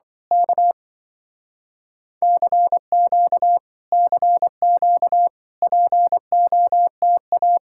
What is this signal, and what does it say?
Morse code, heard as a keyed tone, about 24 wpm, 700 Hz. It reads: K CQ CQ POTA
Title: Speed-Racing Common QSO Phrases Encore - 20wpm